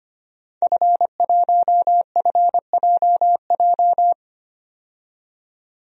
Morse code, heard as a keyed tone, about 25 wpm, 700 Hz. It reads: F1FJJ
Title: DXCC-100 Call Signs - 5 Characters - 25wpm